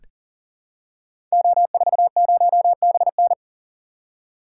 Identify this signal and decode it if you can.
Morse code, heard as a keyed tone, about 39 wpm, 700 Hz. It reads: O40BN
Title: Code Groups - 5 Letter:Numbers - 40wpm